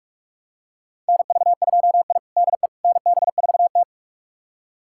Morse code, heard as a keyed tone, about 42 wpm, 700 Hz. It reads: NV2I DE NB4T
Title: USA Callsign DE Callsign - 4 Characters - 45wpm